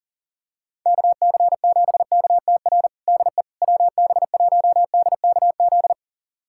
Morse code, heard as a keyed tone, about 40 wpm, 700 Hz. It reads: KC7KTR DE WB1DKZ
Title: USA Callsign DE Callsign 40wpm